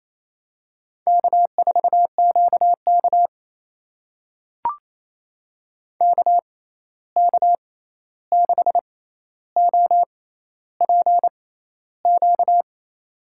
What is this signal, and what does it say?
Morse code, heard as a keyed tone, about 28 wpm, spaced as for 13 wpm, 700 Hz. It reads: K4QK KK6OPQ
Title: USA Call Signs - 6x Character Spacing - 28wpm